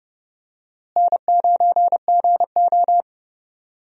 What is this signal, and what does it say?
Morse code, heard as a keyed tone, about 30 wpm, 700 Hz. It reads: N9GO
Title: USA Call Signs - 4 Characters - Repeated 3x - 30wpm